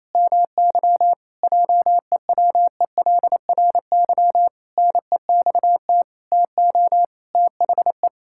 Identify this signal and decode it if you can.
Morse code, heard as a keyed tone, about 28 wpm, 700 Hz. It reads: MY JEWELRY NEXT TO THE
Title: Pangram Challenge v2 - 28wpm